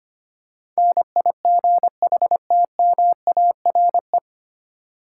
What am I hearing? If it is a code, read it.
Morse code, heard as a keyed tone, about 25 wpm, 700 Hz. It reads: NIGHTMARE
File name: practice / 25wpm / 9 Letter Words - First 3 Letters Given - 25wpm